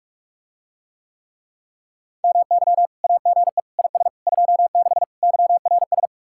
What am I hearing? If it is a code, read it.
Morse code, heard as a keyed tone, about 49 wpm, 700 Hz. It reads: MY AGE IS 26 YRS